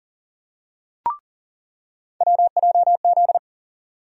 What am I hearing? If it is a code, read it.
Morse code, heard as a keyed tone, about 40 wpm, 700 Hz. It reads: WJZ
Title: USA Call Sign Postfixes - 3 Letters 40wpm